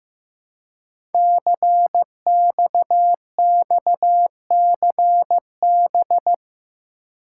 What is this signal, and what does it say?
Morse code, heard as a keyed tone, about 15 wpm, 700 Hz. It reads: CXXCB
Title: Code Groups - 5 Letters - 15wpm